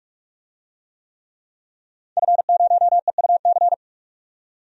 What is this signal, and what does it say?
Morse code, heard as a keyed tone, about 45 wpm, 700 Hz. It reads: F0EUC
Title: DXCC-100 Call Signs - 5 Characters - 45wpm